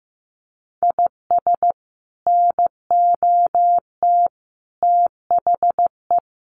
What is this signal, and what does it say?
Morse code, heard as a keyed tone, about 15 wpm, 700 Hz. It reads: IS NOT THE
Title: Sets of 5 Words Encore 15wpm